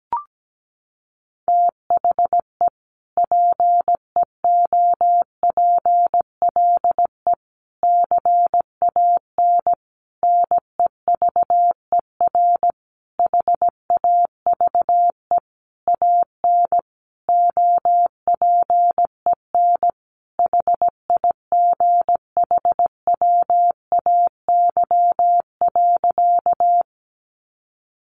Morse code, heard as a keyed tone, about 17 wpm, 700 Hz. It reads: THE PEOPLE CAN NEVER HAVE AN OPEN HIGHWAY.
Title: Sentences from Top 1400 Words 17wpm